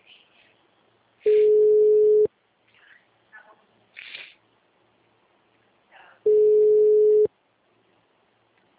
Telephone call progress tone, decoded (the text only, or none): ringback tone